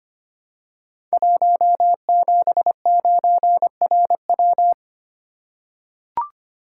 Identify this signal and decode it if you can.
Morse code, heard as a keyed tone, about 26 wpm, 700 Hz. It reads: 179RW
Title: Code Groups - 5 Letter:Numbers - 25wpm